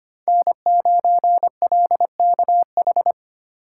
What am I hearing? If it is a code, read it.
Morse code, heard as a keyed tone, about 25 wpm, 700 Hz. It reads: N9LKH
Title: USA Call Signs - 5 Characters - 4x Character Spacing - 25wpm